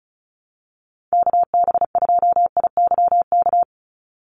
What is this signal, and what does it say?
Morse code, heard as a keyed tone, about 34 wpm, 700 Hz. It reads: KB2IYK
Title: USA Call Signs 35wpm